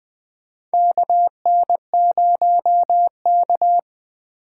Morse code, heard as a keyed tone, about 20 wpm, 700 Hz. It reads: KN0K
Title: USA Call Signs - 4 Characters - 2x Character Spacing - 20wpm